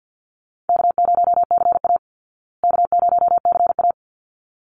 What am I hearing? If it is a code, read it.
Morse code, heard as a keyed tone, about 50 wpm, 700 Hz. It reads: K0CA K0CA
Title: Easy Call Signs Repeated 2x - 50wpm